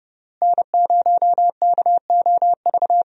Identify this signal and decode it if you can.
Morse code, heard as a keyed tone, about 30 wpm, 700 Hz. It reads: N0KOV
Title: USA Call Signs - 5 Characters - Repeated 3x - 30wpm